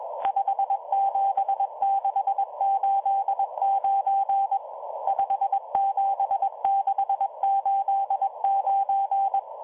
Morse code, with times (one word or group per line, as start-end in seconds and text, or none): 0.3-4.6 57689
5.1-9.4 57689